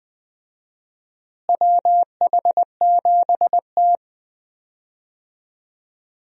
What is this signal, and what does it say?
Morse code, heard as a keyed tone, about 21 wpm, 700 Hz. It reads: WH7T
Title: USA Call Signs - 4 Characters - Repeated 2x - 20wpm